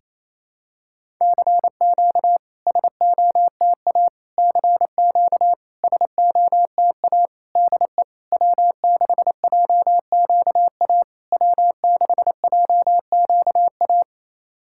Morse code, heard as a keyed tone, about 28 wpm, 700 Hz. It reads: CQ SOTA CQ SOTA DE W6JQA W6JQA
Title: SOTA QSO Practice 28wpm